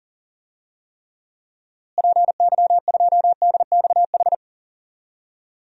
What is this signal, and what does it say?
Morse code, heard as a keyed tone, about 37 wpm, 700 Hz. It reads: PY2DXH
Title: DXCC-100 Call Signs - 6 Characters - Repeated 3x - 40wpm